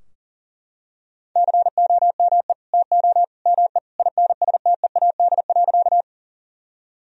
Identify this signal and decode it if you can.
Morse code, heard as a keyed tone, about 40 wpm, 700 Hz. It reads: COME TO ME INSTEAD.
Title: Sentences from Top 1000 Words - Review Part 4 - 40wpm